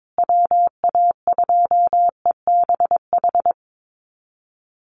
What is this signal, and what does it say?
Morse code, heard as a keyed tone, about 22 wpm, 700 Hz. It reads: WA2EBH
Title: USA Call Signs Repeated 2x 22wpm